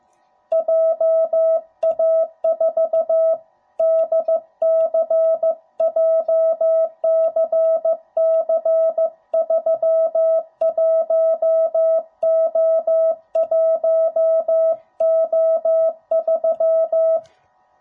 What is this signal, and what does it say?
Morse code, heard as a keyed tone, about 15 wpm, 650 Hz. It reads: JA4DC JCC31O1O3
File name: jh8cbh / 2021cw / cw05b